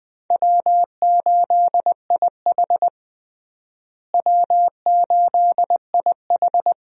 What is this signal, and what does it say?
Morse code, heard as a keyed tone, about 20 wpm, 700 Hz. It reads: W8IH W8IH
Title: USA Call Signs - 4 Characters - Repeated 2x - 20wpm